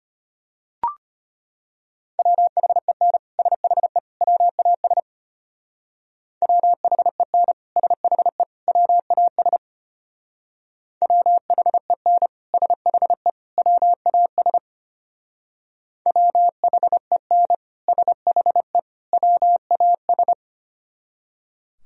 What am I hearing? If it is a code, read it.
Morse code, heard as a keyed tone, about 32 wpm, 700 Hz. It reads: WHEN SHE WAS WHEN SHE WAS WHEN SHE WAS WHEN SHE WAS